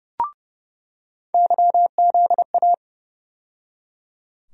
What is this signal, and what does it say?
Morse code, heard as a keyed tone, about 29 wpm, 700 Hz. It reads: YZA
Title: USA Call Sign Postfixes - 3 Letters 30wpm